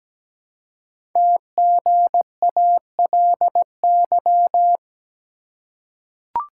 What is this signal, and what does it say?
Morse code, heard as a keyed tone, about 17 wpm, 700 Hz. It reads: TGALY
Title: Code Groups - 5 Letters - 17wpm